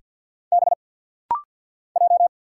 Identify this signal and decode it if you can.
Morse code, heard as a keyed tone, about 50 wpm, 690 Hz. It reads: B J
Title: Single Letter-Number Alt - Rapid-Fire 50wpm